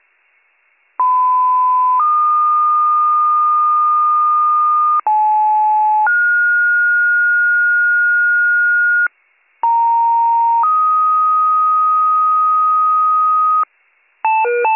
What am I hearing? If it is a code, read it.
Two-tone page, heard as a one-second tone, then a three-second tone, about 1010 then 1220 Hz, about 860 then 1430 Hz, about 930 then 1250 Hz. It is heard from 1.0 to 13.6 s.